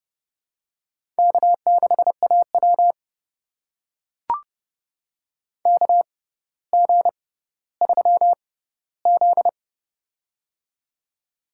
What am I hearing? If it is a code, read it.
Morse code, heard as a keyed tone, about 29 wpm, spaced as for 15 wpm, 700 Hz. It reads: K6AW KG3Z
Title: USA Call Signs - 6x Character Spacing - 30wpm